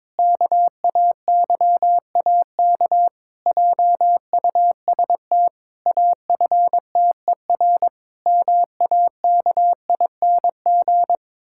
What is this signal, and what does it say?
Morse code, heard as a keyed tone, about 22 wpm, 700 Hz. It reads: KAYAK JUST AFTER MAKING
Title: Pangram Challenge v2 - 22wpm